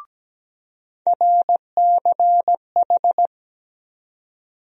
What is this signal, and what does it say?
Morse code, heard as a keyed tone, about 17 wpm, 700 Hz. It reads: RCH
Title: USA Call Sign Postfixes - 3 Letters 17wpm